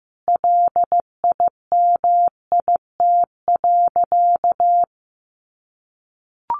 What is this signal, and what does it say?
Morse code, heard as a keyed tone, about 15 wpm, 700 Hz. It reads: LIMIT.
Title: Sentences from Top 1500 Words 15wpm